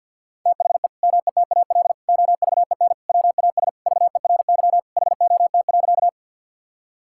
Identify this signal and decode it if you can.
Morse code, heard as a keyed tone, about 50 wpm, 700 Hz. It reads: THE METAL OVEN WAS VERY HOT.
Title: Sentences from Top 1000 Words - Review Part 6 - 50wpm